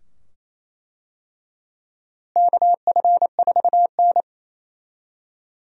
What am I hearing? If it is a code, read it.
Morse code, heard as a keyed tone, about 28 wpm, 700 Hz. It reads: KF4N
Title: USA Call Signs - 4 Characters - Repeated 2x - 28wpm